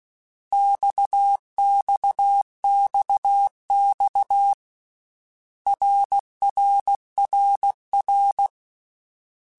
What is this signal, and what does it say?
Morse code, heard as a keyed tone, about 16 wpm, 800 Hz. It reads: XXXX RRRR
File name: Basic Letter Groups By7Groups / RKPX / RKPX@13